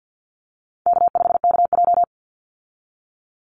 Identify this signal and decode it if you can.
Morse code, heard as a keyed tone, about 49 wpm, 700 Hz. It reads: K5KJ
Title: Call Signs - 50wpm